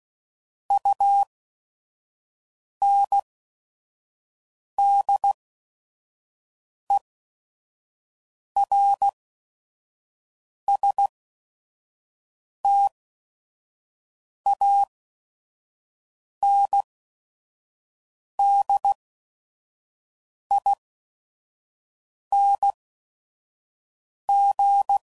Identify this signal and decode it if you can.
Morse code, heard as a keyed tone, about 16 wpm, spaced as for 5 wpm, 800 Hz. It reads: UNDERSTANDING
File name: CW Code Groups @5WPM / Technical Subjects / SwPwrSup